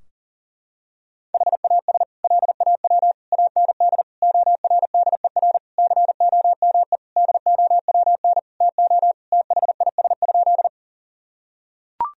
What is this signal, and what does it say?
Morse code, heard as a keyed tone, about 40 wpm, 700 Hz. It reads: HAS LAW AND ORDER COME DOWN TO THIS?
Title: Sentences from Top 1000 Words - Review Part 3 - 40wpm